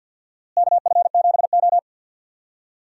Morse code, heard as a keyed tone, about 52 wpm, 700 Hz. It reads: KF7O
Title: USA Call Signs - 4 Characters - Repeated 3x - 50wpm